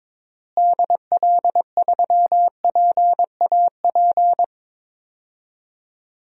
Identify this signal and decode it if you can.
Morse code, heard as a keyed tone, about 22 wpm, 700 Hz. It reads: DL3PAP
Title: DXCC-100 Call Signs - 6 Characters - Repeated 2x - 22wpm